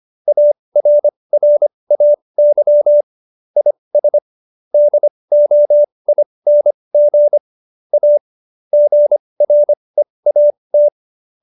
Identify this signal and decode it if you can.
Morse code, heard as a keyed tone, about 25 wpm, spaced as for 20 wpm, 590 Hz. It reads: ARRAY IS DOING A GREAT